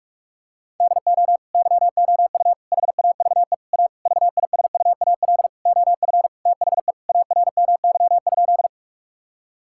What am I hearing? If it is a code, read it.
Morse code, heard as a keyed tone, about 46 wpm, 690 Hz. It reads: DO YOU HAVE A VISUAL OF THE ARMY?
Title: Sentences from Top 1000 Words - Review Part 6 - 45wpm